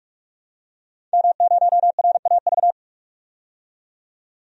Morse code, heard as a keyed tone, about 42 wpm, 700 Hz. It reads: M0RAV
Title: DXCC-100 Call Signs - 5 Characters - 45wpm